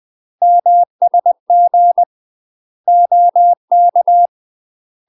Morse code, heard as a keyed tone, about 20 wpm, 700 Hz. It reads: MSG OK
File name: UK MASTER QSOs TRAINING / 20WPM / UK MASTER QSOs 0058